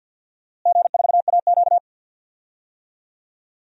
Morse code, heard as a keyed tone, about 52 wpm, 700 Hz. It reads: G4AQ
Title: DXCC-100 Call Signs - 4 Characters - Repeated 3x - 50wpm